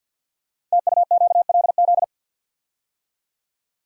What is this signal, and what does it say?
Morse code, heard as a keyed tone, about 52 wpm, 700 Hz. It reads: TUQLZ
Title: Code Groups - 5 Letters - 50wpm